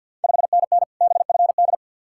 Morse code, heard as a keyed tone, about 53 wpm, 690 Hz. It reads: 5NN BFD
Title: Ontario QSO Party Exchange 50wpm